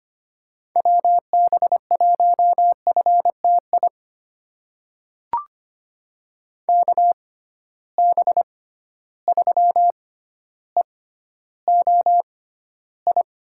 Morse code, heard as a keyed tone, about 25 wpm, spaced as for 13 wpm, 700 Hz. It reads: WB1FTI KB3EOI
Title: USA Call Signs - 6 Characters - 6x Character Spacing - 25wpm